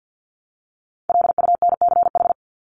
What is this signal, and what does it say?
Morse code, heard as a keyed tone, about 49 wpm, 700 Hz. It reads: LUNCH
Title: Top 1751-2000 Words 50wpm